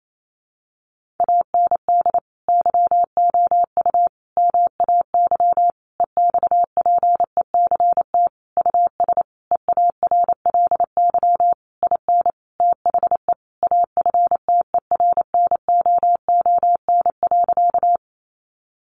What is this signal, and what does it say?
Morse code, heard as a keyed tone, about 28 wpm, 700 Hz. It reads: AND YOU MAY EXPECT US EARLY IN THE AFTERNOON.
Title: Sentences from Top 1500 Words 28wpm